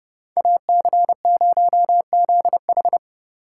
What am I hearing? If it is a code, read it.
Morse code, heard as a keyed tone, about 36 wpm, 700 Hz. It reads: AC0ZH
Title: Speed Racing - USA 2x2 Call Signs 30wpm